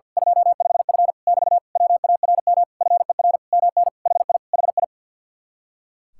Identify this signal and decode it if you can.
Morse code, heard as a keyed tone, about 50 wpm, 700 Hz. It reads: 25F <BT> WARM FER MN HI HI
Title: Speed-Racing Common QSO Phrases - 50wpm